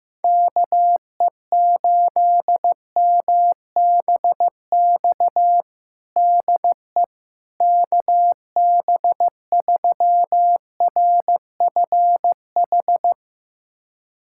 Morse code, heard as a keyed tone, about 15 wpm, 700 Hz. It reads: KE8MBX DE KB3RFH
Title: USA Callsign DE Callsign 15wpm